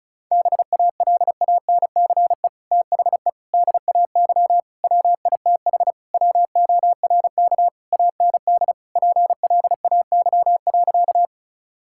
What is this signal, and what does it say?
Morse code, heard as a keyed tone, about 36 wpm, 700 Hz. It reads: BALANCE THE DAY WITH WORK AND PLAY.
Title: Sentences from Top 1000 Words - Review Part 6 - 35wpm